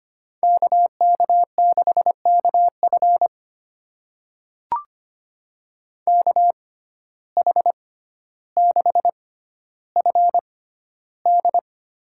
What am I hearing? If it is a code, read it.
Morse code, heard as a keyed tone, about 25 wpm, spaced as for 13 wpm, 700 Hz. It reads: KK6KF KH6FD
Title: USA Call Signs - 6x Character Spacing - 25wpm